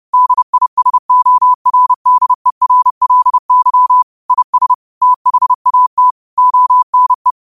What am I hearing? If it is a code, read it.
Morse code, heard as a keyed tone, about 30 wpm, 1000 Hz. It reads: DISORDERLY IS THAT ONE